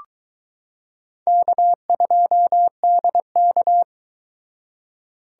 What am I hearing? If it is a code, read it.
Morse code, heard as a keyed tone, about 22 wpm, 700 Hz. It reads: K2DK